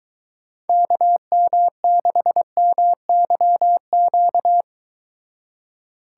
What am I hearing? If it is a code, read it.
Morse code, heard as a keyed tone, about 23 wpm, 700 Hz. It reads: KM6MYQ